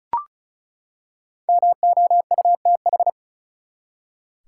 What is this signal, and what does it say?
Morse code, heard as a keyed tone, about 36 wpm, 700 Hz. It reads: MOUTH
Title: Five Letter Words - 35wpm